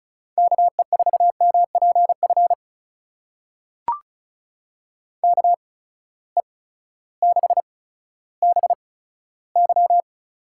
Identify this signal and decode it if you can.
Morse code, heard as a keyed tone, about 36 wpm, spaced as for 16 wpm, 700 Hz. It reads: KE4MPF KE6BY
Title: USA Call Signs - 6 Characters - 8x Character Spacing - 35wpm